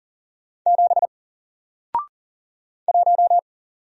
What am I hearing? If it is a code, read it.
Morse code, heard as a keyed tone, about 40 wpm, 700 Hz. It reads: 7 1
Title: Single Letter-Number Alt 40wpm